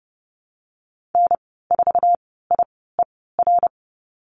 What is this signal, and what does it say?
Morse code, heard as a keyed tone, about 30 wpm, spaced as for 17 wpm, 700 Hz. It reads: N4IER
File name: Lesson 14g - Number 2 - Call Signs 30wpm